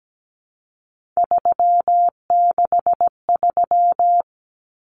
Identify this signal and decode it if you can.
Morse code, heard as a keyed tone, about 17 wpm, 700 Hz. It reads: 363
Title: Common Cut Numbers - 3-Digit 17wpm